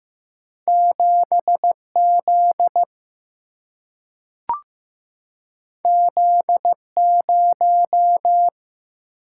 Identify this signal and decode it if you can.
Morse code, heard as a keyed tone, about 15 wpm, 700 Hz. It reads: 7 Z Z 0
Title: Single-Letter Single-Number 15wpm